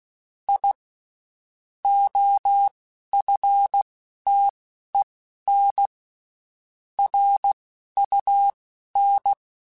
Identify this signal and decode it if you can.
Morse code, heard as a keyed tone, about 16 wpm, spaced as for 12 wpm, 800 Hz. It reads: I OFTEN RUN
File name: (spaced) / QSOs Spaced-mp3s / kc7sac @15 sp